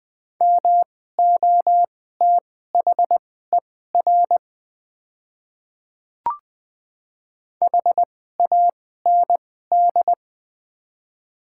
Morse code, heard as a keyed tone, about 20 wpm, spaced as for 14 wpm, 700 Hz. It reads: MOTHER HAND
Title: 2x Character Spacing - Top 200 Words - 20wpm